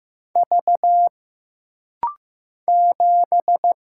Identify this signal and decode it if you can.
Morse code, heard as a keyed tone, about 15 wpm, 700 Hz. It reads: V 7
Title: Single Letter-Number Alt - Rapid-Fire 15wpm